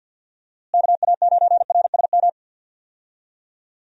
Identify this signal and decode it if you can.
Morse code, heard as a keyed tone, about 52 wpm, 700 Hz. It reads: KA9RSM
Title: USA Call Signs - 6 Characters - Repeated 3x - 50wpm